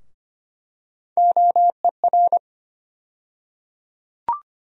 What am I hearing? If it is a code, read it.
Morse code, heard as a keyed tone, about 25 wpm, 700 Hz. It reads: OER
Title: USA Call Sign Postfixes - 3 Letters 25wpm